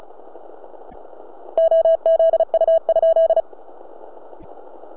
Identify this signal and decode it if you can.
Morse code, heard as a keyed tone, about 36 wpm, 650 Hz. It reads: OZU?